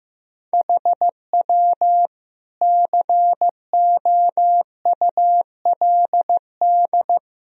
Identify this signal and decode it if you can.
Morse code, heard as a keyed tone, about 16 wpm, 700 Hz. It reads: HW COULD
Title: Sentences from Top 400 Words and Abbreviations 15wpm